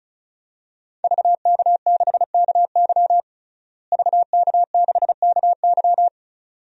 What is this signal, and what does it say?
Morse code, heard as a keyed tone, about 34 wpm, 700 Hz. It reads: VK6KY VK6KY
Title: DXCC-100 Call Signs - 5 Characters - Repeated 2x - 35wpm